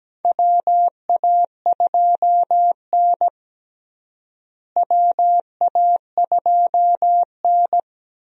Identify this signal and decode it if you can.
Morse code, heard as a keyed tone, about 17 wpm, 700 Hz. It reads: WA2N WA2N
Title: USA Call Signs - 4 Characters - Repeated 2x - 17wpm